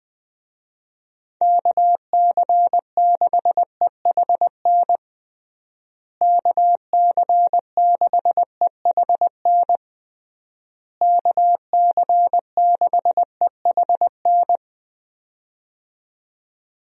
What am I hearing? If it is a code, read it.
Morse code, heard as a keyed tone, about 21 wpm, 700 Hz. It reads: KC6EHN KC6EHN KC6EHN
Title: USA Call Signs - 6 Characters - Repeated 3x - 20wpm